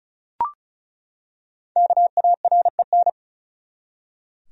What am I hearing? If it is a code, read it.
Morse code, heard as a keyed tone, about 36 wpm, 700 Hz. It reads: KAREN
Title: USA Common Names - Max 5 Letters - 35wpm